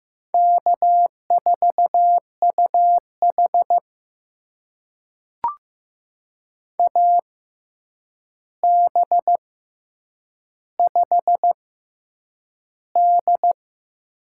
Morse code, heard as a keyed tone, about 15 wpm, 700 Hz. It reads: K 4 U H A B 5 D
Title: USA Call Signs - 4 Characters - 6x Character Spacing - 15wpm